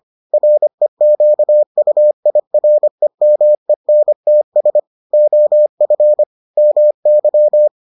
Morse code, heard as a keyed tone, about 25 wpm, 600 Hz. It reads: REQUIREMENTS OF MY